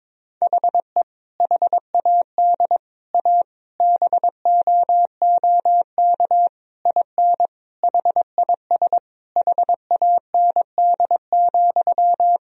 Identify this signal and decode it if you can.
Morse code, heard as a keyed tone, about 22 wpm, 700 Hz. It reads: HE HAD A BOOK IN HIS HAND,
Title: Sentences from Top 1000 Words - Review Part 3 - 22wpm